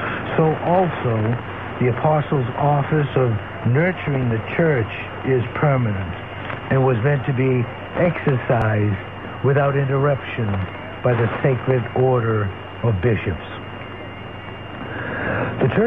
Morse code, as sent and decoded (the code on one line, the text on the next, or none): none